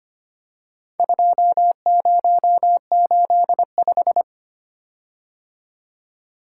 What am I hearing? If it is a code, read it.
Morse code, heard as a keyed tone, about 25 wpm, 700 Hz. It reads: 2085
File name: practice / 25wpm / Four Letter-Number Combinations - 25wpm